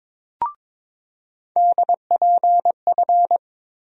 Morse code, heard as a keyed tone, about 21 wpm, 700 Hz. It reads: DPF
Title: USA Call Sign Postfixes - 3 Letters 22wpm